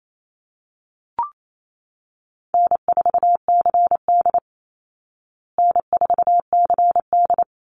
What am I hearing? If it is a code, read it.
Morse code, heard as a keyed tone, about 28 wpm, 700 Hz. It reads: N4CD N4CD
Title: Call Signs Repeated 3x - 28wpm